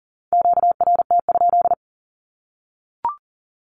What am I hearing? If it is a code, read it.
Morse code, heard as a keyed tone, about 40 wpm, 700 Hz. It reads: QRT?
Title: Q-Codes Repeated 3x 40wpm